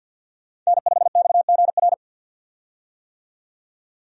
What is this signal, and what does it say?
Morse code, heard as a keyed tone, about 53 wpm, 700 Hz. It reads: N5XGR
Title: USA Call Signs - 5 Characters - 50wpm